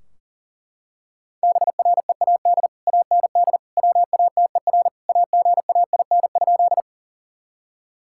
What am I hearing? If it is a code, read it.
Morse code, heard as a keyed tone, about 37 wpm, 700 Hz. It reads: BREAD AND WATER AGAIN?
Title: Sentences from Top 1600 Words 40wpm